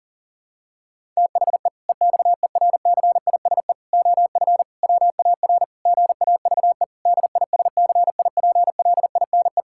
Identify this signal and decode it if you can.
Morse code, heard as a keyed tone, about 37 wpm, 700 Hz. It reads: THE EXERCISE OF WAR GAVE DISCIPLINE
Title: Sentences from Top 1000 Words - Review Part 6 - 40wpm